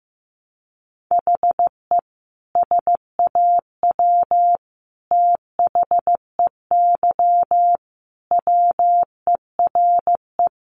Morse code, heard as a keyed tone, about 15 wpm, 700 Hz. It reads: HE SAW THEY WERE
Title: Sentences from Top 700 Words 15wpm